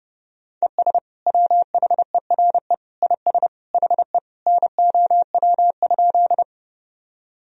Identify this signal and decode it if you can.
Morse code, heard as a keyed tone, about 30 wpm, 700 Hz. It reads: ES WHERE IS HE NOW?
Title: Sentences from Top 200 Words and Abbreviations 30wpm